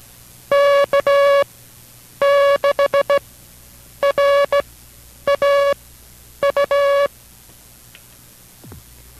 Morse code, heard as a keyed tone, about 16 wpm, spaced as for 8 wpm, 560 Hz. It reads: K6RAU
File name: kj6art / K6RAU MP3s / 09-XZ